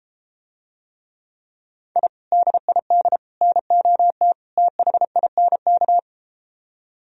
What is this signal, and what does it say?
Morse code, heard as a keyed tone, about 33 wpm, 700 Hz. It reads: I DID NOT THINK